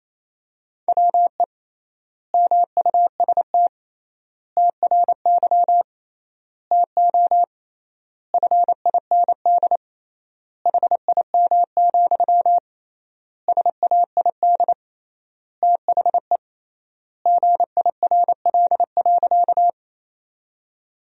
Morse code, heard as a keyed tone, about 28 wpm, 700 Hz. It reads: WE MUST TRY TO FIND HIM, SAID THE GIRL.
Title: 3x Word-Spacing - Sentences from Top 400 Words - 28wpm